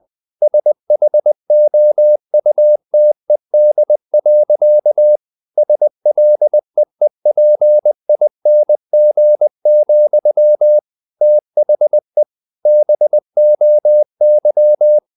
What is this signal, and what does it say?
Morse code, heard as a keyed tone, about 20 wpm, 600 Hz. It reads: SHOUTED. SLEEPING, THE BOY
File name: the old man and the sea part 2 20wpm